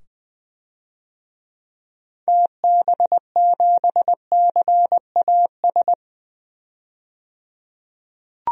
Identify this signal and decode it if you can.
Morse code, heard as a keyed tone, about 21 wpm, 700 Hz. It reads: TB7CAS
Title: DXCC-100 Call Signs - 6 Characters - 20wpm